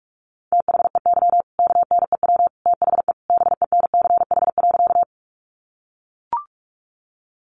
Morse code, heard as a keyed tone, about 44 wpm, 700 Hz. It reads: THEY KNEW THE BENCH.
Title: Sentences from Top 1400 Words 45wpm